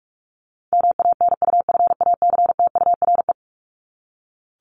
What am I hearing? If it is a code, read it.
Morse code, heard as a keyed tone, about 44 wpm, 700 Hz. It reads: MANUFACTURE
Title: ING Suffix - Top 100 Words with Base Words 45wpm